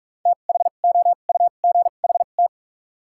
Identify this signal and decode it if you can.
Morse code, heard as a keyed tone, about 44 wpm, spaced as for 32 wpm, 700 Hz. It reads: THOUGHT
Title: 2x Character Spacing - Top 300 Words - 45wpm